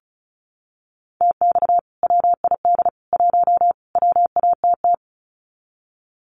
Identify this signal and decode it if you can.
Morse code, heard as a keyed tone, about 34 wpm, 700 Hz. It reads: TX WID 1 WATT
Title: Common QSO Phrases 35wpm